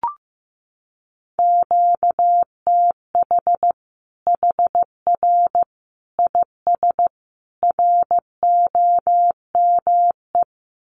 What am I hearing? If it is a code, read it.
Morse code, heard as a keyed tone, about 15 wpm, 700 Hz. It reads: QTH HR IS ROME
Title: Common QSO Phrases 15wpm